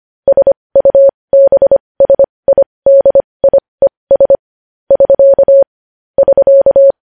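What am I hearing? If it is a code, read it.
Morse code, heard as a keyed tone, about 25 wpm, spaced as for 20 wpm, 550 Hz. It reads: SUBSIDIES <SK> <SK>
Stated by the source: Morse news headlines sent at 20, opening with vvv.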